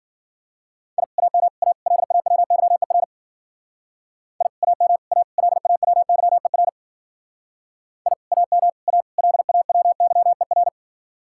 Morse code, heard as a keyed tone, about 48 wpm, 700 Hz. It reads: I AM A LAWYER I AM A LAWYER I AM A LAWYER
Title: Speed-Racing Common QSO Phrases Encore - 40wpm